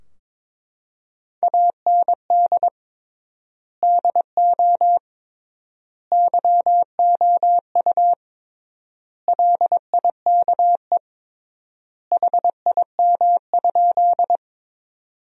Morse code, heard as a keyed tone, about 22 wpm, 700 Hz. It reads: AND DO YOU LIKE HIM?
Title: 3x Word-Spacing - Sentences from Top 100 Words - 22wpm